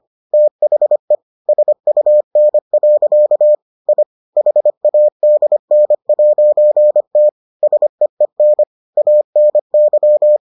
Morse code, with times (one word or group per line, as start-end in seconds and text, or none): 0.3-1.1 THE
1.5-3.5 SUN.
3.9-4.0 I
4.4-7.3 HADN'T
7.6-8.6 SEEN
9.0-10.4 ANY